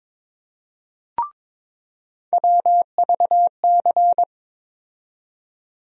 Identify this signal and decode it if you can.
Morse code, heard as a keyed tone, about 22 wpm, 700 Hz. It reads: WVC
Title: USA Call Sign Postfixes - 3 Letters 22wpm